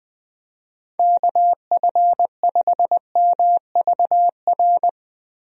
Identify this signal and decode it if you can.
Morse code, heard as a keyed tone, about 21 wpm, 700 Hz. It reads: KF5MVR
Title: USA Call Signs - 6 Characters - 20wpm